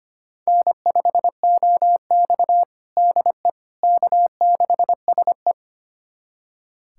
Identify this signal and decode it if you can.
Morse code, heard as a keyed tone, about 25 wpm, 700 Hz. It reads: N5OX DE K6SE
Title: USA Callsign DE Callsign - 4 Characters - 25wpm